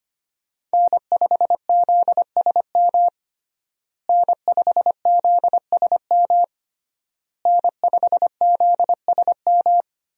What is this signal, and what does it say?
Morse code, heard as a keyed tone, about 25 wpm, 700 Hz. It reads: N5ZSM N5ZSM N5ZSM
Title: USA Call Signs - 5 Characters - Repeated 3x - 25wpm